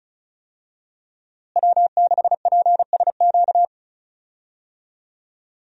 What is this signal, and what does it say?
Morse code, heard as a keyed tone, about 36 wpm, 700 Hz. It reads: W6PSQ
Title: USA Call Signs - 5 Characters - Repeated 3x - 35wpm